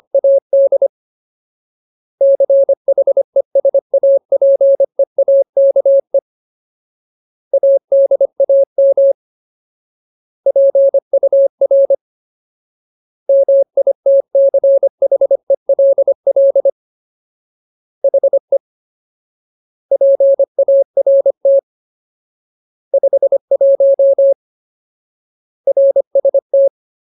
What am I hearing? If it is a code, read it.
Morse code, heard as a keyed tone, about 25 wpm, 550 Hz. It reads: AD CHESAPEAKE ADAM PUR MITCHELL HE PART 51 RST
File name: Ultimate Word List / 25WPM Ultimate